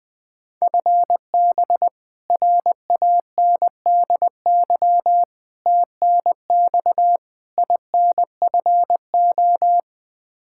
Morse code, heard as a keyed tone, about 21 wpm, 700 Hz. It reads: FB RANDY TNX INFO
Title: K1USN SST QSO Practice 20wpm